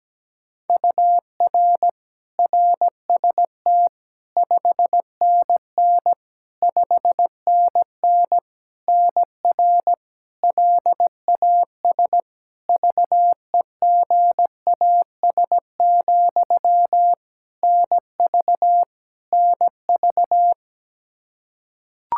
Morse code, heard as a keyed tone, about 17 wpm, 700 Hz. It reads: UR RST 5NN 5NN NR LAS VEGAS, NV NV
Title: K1USN SST QSO Practice 17wpm